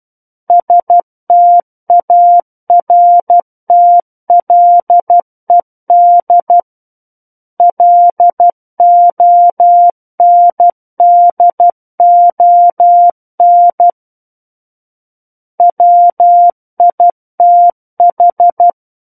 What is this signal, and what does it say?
Morse code, heard as a keyed tone, about 12 wpm, 700 Hz. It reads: STARTLED LONDON WITH
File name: Book One / 03-Chapter 3 - 12 WPM